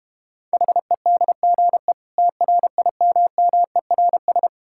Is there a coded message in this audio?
Yes: Morse code from 0.5 to 4.5 s, heard as a keyed tone, about 32 wpm, 700 Hz. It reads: HEDGE TRIMMERS